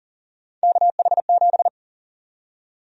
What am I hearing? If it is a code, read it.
Morse code, heard as a keyed tone, about 40 wpm, 700 Hz. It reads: KH7
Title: USA Call Sign Prefixes - 40wpm